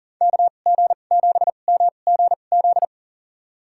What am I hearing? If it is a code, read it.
Morse code, heard as a keyed tone, about 40 wpm, spaced as for 29 wpm, 700 Hz. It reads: KG7MGZ
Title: USA Call Signs - 6 Characters - 2x Character Spacing - 40wpm